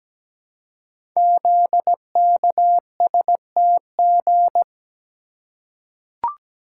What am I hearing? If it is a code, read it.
Morse code, heard as a keyed tone, about 17 wpm, 700 Hz. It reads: ZKSTG
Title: Code Groups - 5 Letters - 17wpm